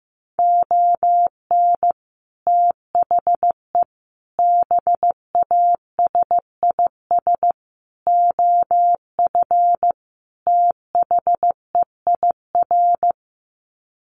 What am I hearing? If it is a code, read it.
Morse code, heard as a keyed tone, about 15 wpm, 700 Hz. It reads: ON THE BASIS OF THEIR
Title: Sets of 5 Words Encore 15wpm